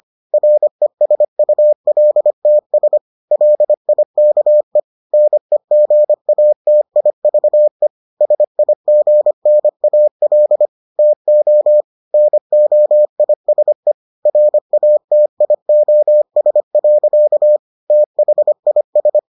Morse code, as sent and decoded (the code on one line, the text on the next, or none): .-. ... ..- .-.. - ... / .-.. .. -.- . / -. --. .- - .. ...- . / ... .. --. -. .- .-.. / - --- / -. --- .. ... / .-. .- - .. --- ... .-.-.- / - .... .. ...
RESULTS LIKE NEGATIVE SIGNAL TO NOISE RATIOS. THIS